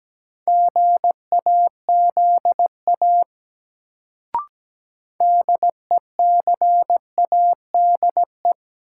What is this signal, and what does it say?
Morse code, heard as a keyed tone, about 17 wpm, 700 Hz. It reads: GAZA DECADE
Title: Sets of 5 Words - New Word Drill - 17wpm